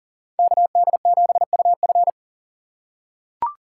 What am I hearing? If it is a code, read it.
Morse code, heard as a keyed tone, about 40 wpm, 700 Hz. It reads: KD7UF
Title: USA Call Signs - 5 Characters - 4x Character Spacing - 40wpm